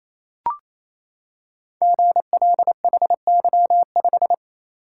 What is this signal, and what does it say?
Morse code, heard as a keyed tone, about 27 wpm, 700 Hz. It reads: GLHY5
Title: Code Groups - 5 Letter:Numbers - 28wpm